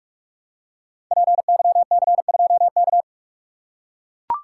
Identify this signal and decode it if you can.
Morse code, heard as a keyed tone, about 44 wpm, 700 Hz. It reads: PYC2K
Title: Code Groups - 5 Letter:Numbers - 45wpm